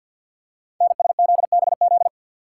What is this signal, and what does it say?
Morse code, heard as a keyed tone, about 48 wpm, 690 Hz. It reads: NSZBZ
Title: Code Groups - 5 Letters - 50wpm